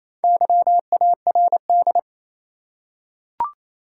Morse code, heard as a keyed tone, about 28 wpm, 700 Hz. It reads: YARD
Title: Four Letter Words - 28wpm